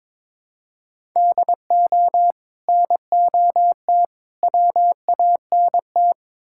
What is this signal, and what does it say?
Morse code, heard as a keyed tone, about 23 wpm, 700 Hz. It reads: DO NOT WANT
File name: Build A Phrase - 5-Word Phrases - 22wpm